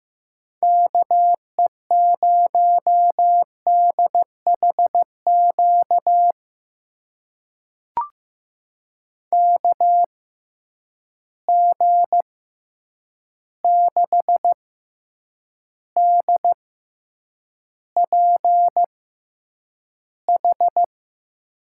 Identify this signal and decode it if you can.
Morse code, heard as a keyed tone, about 15 wpm, spaced as for 8 wpm, 700 Hz. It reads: KE0DHQ KG6DPH
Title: USA Call Signs - 6 Characters - 6x Character Spacing - 15wpm